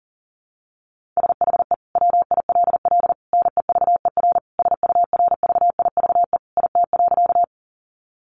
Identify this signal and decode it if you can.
Morse code, heard as a keyed tone, about 40 wpm, 700 Hz. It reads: SHE WILL NEVER SURVIVE IT.
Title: Sentences from Top 1300 Words 40wpm